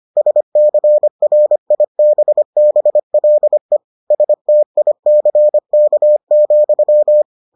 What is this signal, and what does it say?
Morse code, heard as a keyed tone, about 25 wpm, 600 Hz. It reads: SCRIBBLE STICK,